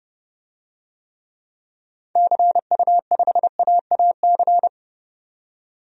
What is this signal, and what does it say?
Morse code, heard as a keyed tone, about 30 wpm, 700 Hz. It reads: CU5AAC
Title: DXCC-100 Call Signs - 6 Characters - 30wpm